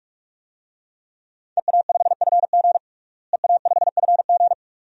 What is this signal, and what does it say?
Morse code, heard as a keyed tone, about 46 wpm, 700 Hz. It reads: EA5FG EA5FG
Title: DXCC-100 Call Signs - 5 Characters - Repeated 2x - 45wpm